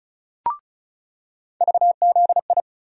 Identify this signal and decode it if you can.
Morse code, heard as a keyed tone, about 34 wpm, 700 Hz. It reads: VZI
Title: USA Call Sign Postfixes - 3 Letters 35wpm